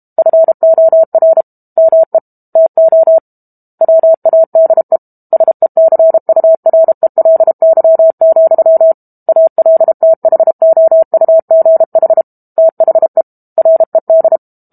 Morse code, heard as a keyed tone, about 32 wpm, 650 Hz. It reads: FOR ME TO WADE SECURELY, ALTHOUGH THE RED
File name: Book Two / 06-Chapter 6 - 32 WPM